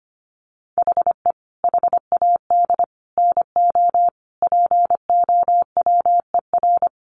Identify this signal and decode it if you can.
Morse code, heard as a keyed tone, about 25 wpm, 700 Hz. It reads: HE HAD NO POWER
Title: Sentences from Top 1400 Words 25wpm